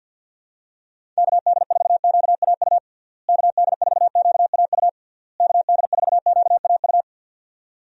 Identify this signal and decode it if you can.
Morse code, heard as a keyed tone, about 50 wpm, 700 Hz. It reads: KD4QAU KD4QAU KD4QAU
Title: USA Call Signs - 6 Characters - Repeated 3x - 50wpm